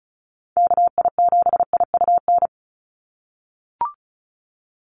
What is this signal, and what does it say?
Morse code, heard as a keyed tone, about 36 wpm, 700 Hz. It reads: KI7IUN
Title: USA Call Signs 35wpm